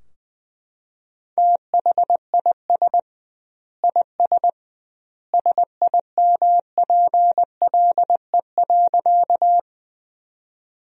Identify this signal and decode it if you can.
Morse code, heard as a keyed tone, about 21 wpm, 700 Hz. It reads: THIS IS SIMPLE.
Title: 2x Word-Spacing - Sentences from Top 400 Words - 20wpm